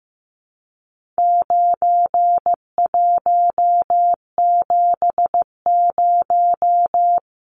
Morse code, heard as a keyed tone, about 15 wpm, 700 Hz. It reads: 9170
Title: Numbers - 4-Digit 15wpm